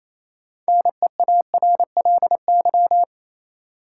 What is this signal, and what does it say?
Morse code, heard as a keyed tone, about 28 wpm, 700 Hz. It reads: NEARLY
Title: Sentences from Top 1300 Words - New Word Drill - 28wpm